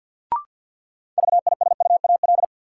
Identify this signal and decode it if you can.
Morse code, heard as a keyed tone, about 48 wpm, 700 Hz. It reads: VISUAL
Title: Sentences from Top 1000 Words - Review Part 6 - New Word Drill - 50wpm